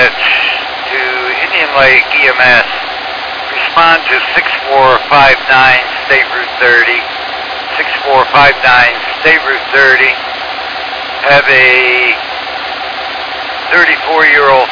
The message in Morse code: none

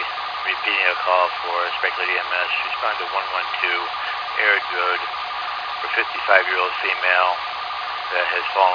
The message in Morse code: none

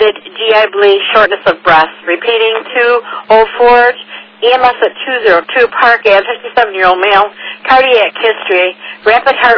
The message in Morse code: none

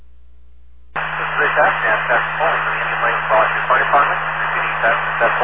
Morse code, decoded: none

